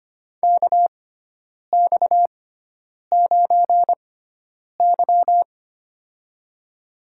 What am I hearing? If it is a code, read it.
Morse code, heard as a keyed tone, about 25 wpm, spaced as for 9 wpm, 700 Hz. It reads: KX9Y
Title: USA Call Signs - 4 Characters - 6x Character Spacing - 25wpm